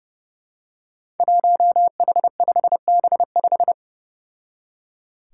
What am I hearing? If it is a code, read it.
Morse code, heard as a keyed tone, about 29 wpm, 700 Hz. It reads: 1H5B5
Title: Five Challenging Letter-Number Combinations 30wpm